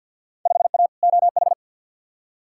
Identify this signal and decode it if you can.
Morse code, heard as a keyed tone, about 53 wpm, 700 Hz. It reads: 5A OH
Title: Field Day Exchanges Encore - 50wpm